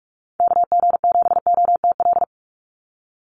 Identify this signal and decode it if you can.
Morse code, heard as a keyed tone, about 46 wpm, 700 Hz. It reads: KG7OTL
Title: USA Call Signs 45wpm